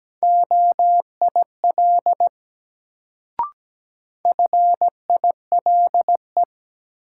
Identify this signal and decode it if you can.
Morse code, heard as a keyed tone, about 17 wpm, 700 Hz. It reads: OIL FILE
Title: Sentences from Top 1000 Words - Review Part 4 - New Word Drill - 17wpm